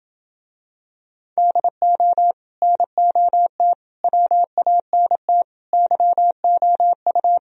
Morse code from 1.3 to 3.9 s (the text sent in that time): DO NOT